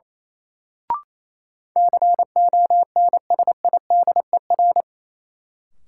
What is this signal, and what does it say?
Morse code, heard as a keyed tone, about 28 wpm, 700 Hz. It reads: CONSIDER
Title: Sentences from Top 600 Words - New Word Drill - 28wpm